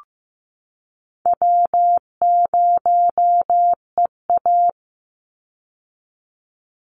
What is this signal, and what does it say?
Morse code, heard as a keyed tone, about 15 wpm, 700 Hz. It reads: W0EA
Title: Call Signs Repeated 3x - 15wpm